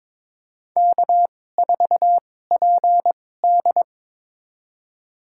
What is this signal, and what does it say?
Morse code, heard as a keyed tone, about 23 wpm, spaced as for 16 wpm, 700 Hz. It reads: K4PD
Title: USA Call Signs - 4 Characters - 2x Character Spacing - 22wpm